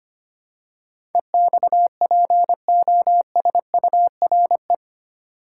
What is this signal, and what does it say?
Morse code, heard as a keyed tone, about 25 wpm, 700 Hz. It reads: EXPOSURE
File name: Eight Letter Words - 25wpm